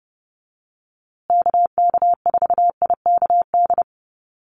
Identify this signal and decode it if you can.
Morse code, heard as a keyed tone, about 30 wpm, 700 Hz. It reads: KK4IKD
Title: USA Call Signs Repeated 2x 30wpm